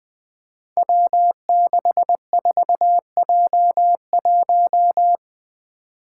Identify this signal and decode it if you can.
Morse code, heard as a keyed tone, about 20 wpm, 700 Hz. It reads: W64J1
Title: Five Challenging Letter-Number Combinations 20wpm